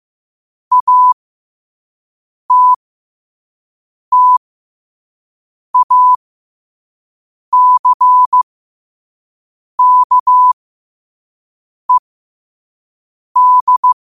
Morse code, heard as a keyed tone, about 15 wpm, spaced as for 5 wpm, 1000 Hz. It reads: ATTACKED
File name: qotdfiles / 5wpm